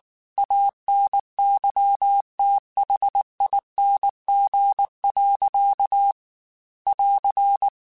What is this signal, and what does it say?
Morse code, heard as a keyed tone, about 19 wpm, 800 Hz. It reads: ANYTHING. <AR>
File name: Quotes / Quotes-mp3s / StandFor @18